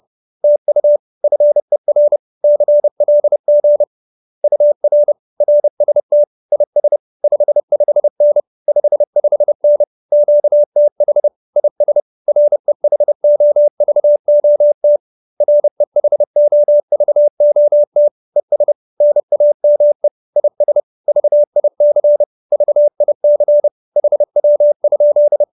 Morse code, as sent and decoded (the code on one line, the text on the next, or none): - ..- / ..-. .-. / -.-. .-.. --. / ..- .-. / .-. ... - / .. ... / ..... ..... -. / ..... ..... -. / --.- - .... / .. ... / .-. .... --- ...- --- - / .-. .... --- ...- --- - / . ... / -. .- -- . / .. ... / ...- .. -.-. / ...- .. -.-. / .... .-- ..--..
TU FER CLG UR RST IS 55N 55N QTH IS REHOVOT REHOVOT ES NAME IS VIC VIC HW?